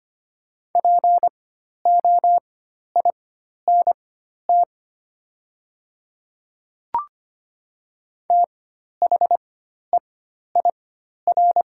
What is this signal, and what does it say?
Morse code, heard as a keyed tone, about 25 wpm, spaced as for 12 wpm, 690 Hz. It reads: POINT THEIR
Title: 4x Character Spacing - Top 500 Words - 25wpm